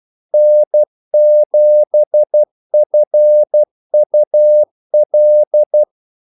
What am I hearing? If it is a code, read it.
Morse code, heard as a keyed tone, about 12 wpm, 600 Hz. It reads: N7FUL
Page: 2 ekstra ordmellemrum mellem kaldesignalerne